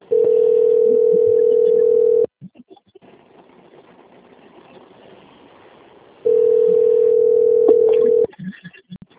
Ringback tone (North America), 440 + 480 Hz.
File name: out- - - - - .52351